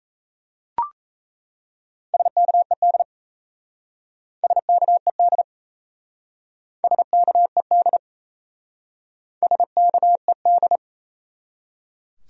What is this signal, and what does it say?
Morse code, heard as a keyed tone, about 37 wpm, 700 Hz. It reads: SKED SKED SKED SKED